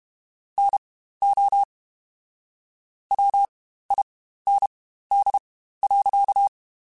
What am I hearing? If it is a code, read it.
Morse code, heard as a keyed tone, about 32 wpm, spaced as for 15 wpm, 800 Hz. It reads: NO WIND.
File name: Spaced Characters / Spaced Characters-mp3s / Weather 3 17 sp @30